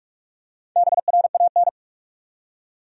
Morse code, heard as a keyed tone, about 46 wpm, 700 Hz. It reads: BRAN
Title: USA Common Names - Max 4 Letters - 45wpm